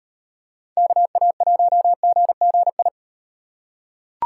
Morse code, heard as a keyed tone, about 35 wpm, 700 Hz. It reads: KA1GGI